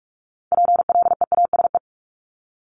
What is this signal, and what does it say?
Morse code, heard as a keyed tone, about 46 wpm, 700 Hz. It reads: PLEASE
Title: Multi-letter Phonemes 45wpm